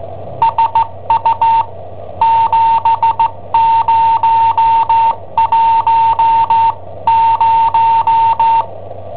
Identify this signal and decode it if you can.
Morse code, heard as a keyed tone, about 17 wpm, spaced as for 15 wpm, 910 Hz. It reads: SU 7010
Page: Cassetta esercizi preparata da I7OHP per IW5ELP